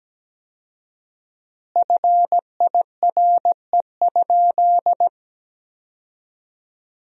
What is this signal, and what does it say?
Morse code, heard as a keyed tone, about 17 wpm, 700 Hz. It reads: FIRE?
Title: 4x Word-Spacing - Sentences from Top 400 Words - 17wpm